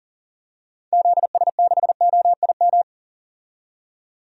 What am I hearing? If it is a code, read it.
Morse code, heard as a keyed tone, about 39 wpm, 700 Hz. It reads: ZS6OIM
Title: DXCC-100 Call Signs - 6 Characters - 40wpm